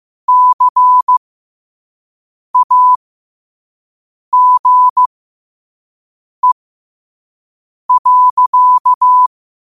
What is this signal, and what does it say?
Morse code, heard as a keyed tone, about 15 wpm, spaced as for 5 wpm, 1000 Hz. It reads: CAGE.
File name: qotdfiles / 5wpm